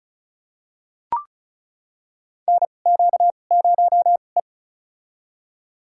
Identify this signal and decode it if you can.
Morse code, heard as a keyed tone, about 34 wpm, spaced as for 25 wpm, 700 Hz. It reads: NQ0E
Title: USA Call Signs - 4 Characters - 2x Character Spacing - 35wpm